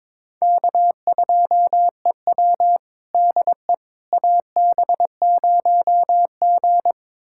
Morse code, heard as a keyed tone, about 21 wpm, 700 Hz. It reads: K2EW DE AB0G
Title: USA Callsign DE Callsign - 4 Characters - 22wpm